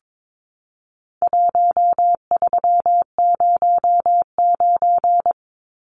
Morse code, heard as a keyed tone, about 22 wpm, 700 Hz. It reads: 1309
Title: Numbers - 4-Digit 22wpm